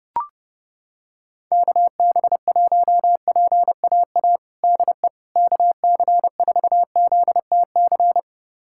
Morse code, heard as a keyed tone, about 30 wpm, 700 Hz. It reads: KB1PAA DE KC4ZTC
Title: USA Callsign DE Callsign - 6 Characters - 30wpm